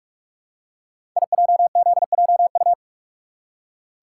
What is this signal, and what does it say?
Morse code, heard as a keyed tone, about 44 wpm, 700 Hz. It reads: IJZJU
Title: Code Groups - 5 Letter:Numbers - 45wpm